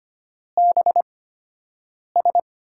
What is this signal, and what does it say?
Morse code, heard as a keyed tone, about 25 wpm, spaced as for 7 wpm, 700 Hz. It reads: BS
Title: USA Call Signs - 8x Character Spacing - 25wpm